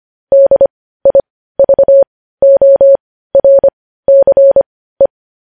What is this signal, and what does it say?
Morse code, heard as a keyed tone, about 25 wpm, spaced as for 15 wpm, 550 Hz. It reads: DIVORCE
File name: podcast / News headlines-1x-25x15